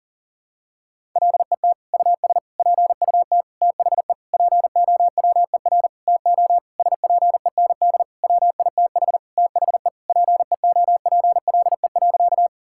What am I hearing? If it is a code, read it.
Morse code, heard as a keyed tone, about 40 wpm, 700 Hz. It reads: LET US PUT THE POWER TO SPEND WITH THE PEOPLE.
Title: Sentences from Top 1000 Words - Review Part 5 - 40wpm